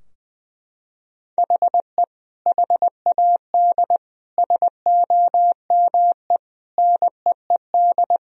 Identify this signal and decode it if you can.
Morse code, heard as a keyed tone, about 20 wpm, 700 Hz. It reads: HE HAD SOME NEED
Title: Sentences from Top 1000 Words - Review Part 3 - 20wpm